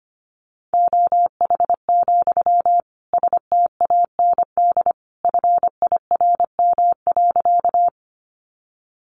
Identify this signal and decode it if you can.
Morse code, heard as a keyed tone, about 25 wpm, 700 Hz. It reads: OH, STAND FIRM.
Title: Sentences from Top 1400 Words 25wpm